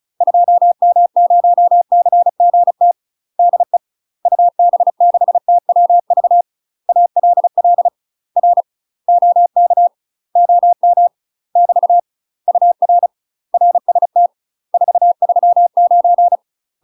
Morse code, heard as a keyed tone, about 36 wpm, 700 Hz. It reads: 2M0CGT DE UB6TWV ALL R OK OM <BT> UR RST 439
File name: UK MASTER QSOs TRAINING / 35WPM / UK MASTER QSOs 0004